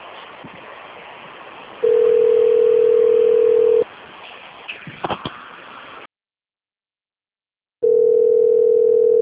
Ringback tone (North America), 440 + 480 Hz.
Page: Rio Claro